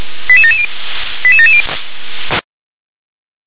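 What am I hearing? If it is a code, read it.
Five-tone selective call (ZVEI), heard as quick runs of five tones, tones of about 70 ms. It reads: 88770 88770
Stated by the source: ZVEI five-tone selective call received on SDR.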